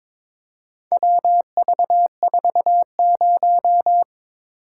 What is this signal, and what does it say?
Morse code, heard as a keyed tone, about 22 wpm, 700 Hz. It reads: WV40
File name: practice / 22wpm / Four Letter-Number Combinations - 22wpm